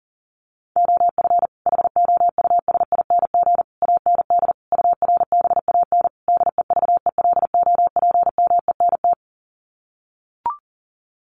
Morse code, heard as a keyed tone, about 40 wpm, 700 Hz. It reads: OF HOUSING AND URBAN DEVELOPMENT
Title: Sets of 5 Words Encore2 40wpm